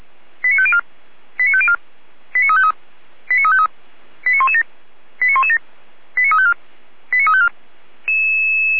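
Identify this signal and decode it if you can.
Five-tone selective call (ZVEI), heard as quick runs of five tones, tones of about 70 ms. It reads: 89504 89504 89353 89353 89117 89117 89354 89354
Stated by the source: ZVEI five-tone call received by SDR, fire brigade Orlamünde (feuerwehr-orlamuende) audio archive.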